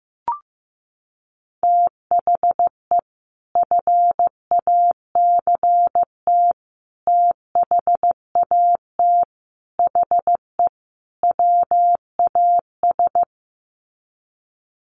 Morse code, heard as a keyed tone, about 15 wpm, 700 Hz. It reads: THE FACT THAT HE WAS
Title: Sets of 5 Words 15wpm